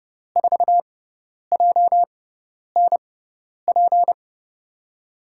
Morse code, heard as a keyed tone, about 30 wpm, spaced as for 10 wpm, 700 Hz. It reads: 4JNP
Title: USA Call Signs - 6x Character Spacing - 30wpm